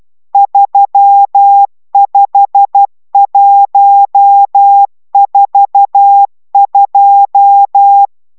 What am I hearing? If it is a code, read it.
Morse code, heard as a keyed tone, about 12 wpm, 800 Hz. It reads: 35142
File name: taviro tanulas / cw 69 22 10